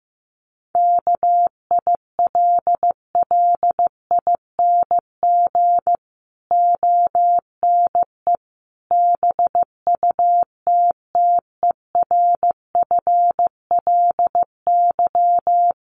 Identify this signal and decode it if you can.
Morse code, heard as a keyed tone, about 15 wpm, 700 Hz. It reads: KILLING ONE BUTTERFLY
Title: A Sound Of Thunder 15wpm